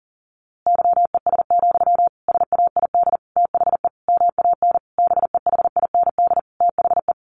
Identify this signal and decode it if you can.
Morse code, heard as a keyed tone, about 40 wpm, 700 Hz. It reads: YES, SAID THE MAN BEHIND THE
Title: A Sound Of Thunder 40wpm